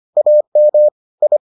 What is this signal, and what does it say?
Morse code, heard as a keyed tone, about 25 wpm, 600 Hz. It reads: AM I